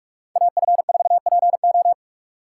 Morse code, heard as a keyed tone, about 46 wpm, 700 Hz. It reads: AF4PO